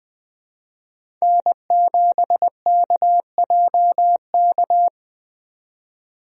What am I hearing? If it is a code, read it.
Morse code, heard as a keyed tone, about 20 wpm, 700 Hz. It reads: N7KJK
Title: USA Call Signs - 5 Characters - 20wpm